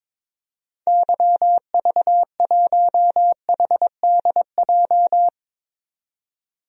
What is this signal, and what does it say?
Morse code, heard as a keyed tone, about 22 wpm, 700 Hz. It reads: YV1HDJ
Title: DXCC-100 Call Signs - 6 Characters - 22wpm